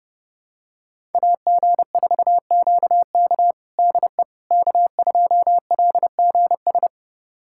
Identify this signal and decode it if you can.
Morse code, heard as a keyed tone, about 30 wpm, 700 Hz. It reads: AG4QK DE K2LGS
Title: USA Callsign DE Callsign - 5 Characters - 30wpm